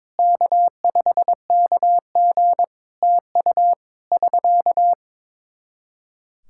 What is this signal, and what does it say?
Morse code, heard as a keyed tone, about 22 wpm, 700 Hz. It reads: K5KG TU <SK>
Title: K1USN SST QSO Practice 22wpm